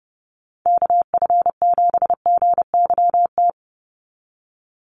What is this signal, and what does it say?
Morse code, heard as a keyed tone, about 30 wpm, 700 Hz. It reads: KF7GYT
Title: USA Call Signs 30wpm